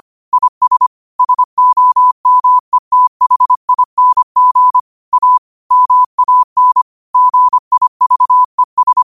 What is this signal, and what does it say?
Morse code, heard as a keyed tone, about 25 wpm, 1000 Hz. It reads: IS SOMETHING A MAN GIVES
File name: qotdfiles / 25wpm